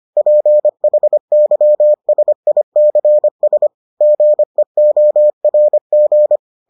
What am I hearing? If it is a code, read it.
Morse code, heard as a keyed tone, about 25 wpm, 600 Hz. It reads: PHYSICS GEORG